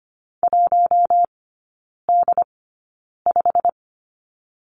Morse code, heard as a keyed tone, about 25 wpm, spaced as for 9 wpm, 700 Hz. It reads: 1D5
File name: Lesson 17e - Letter u - Three Characters 25wpm